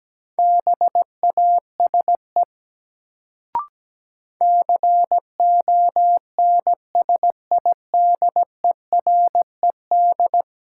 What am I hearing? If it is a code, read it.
Morse code, heard as a keyed tone, about 17 wpm, 700 Hz. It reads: BASE CONSIDERED
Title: Sets of 5 Words - New Word Drill - 17wpm